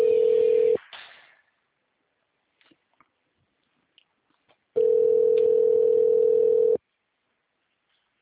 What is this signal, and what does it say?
Ringback tone (North America), 440 + 480 Hz.
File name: out- -2001- - - .34888